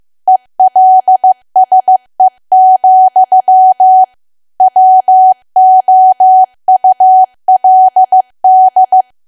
Morse code, heard as a keyed tone, about 15 wpm, 750 Hz. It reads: ELSE, WOULD
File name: cwa / W1AW-Level2 / 15WPM